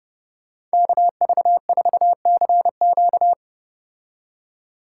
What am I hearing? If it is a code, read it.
Morse code, heard as a keyed tone, about 32 wpm, 700 Hz. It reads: KV4CQ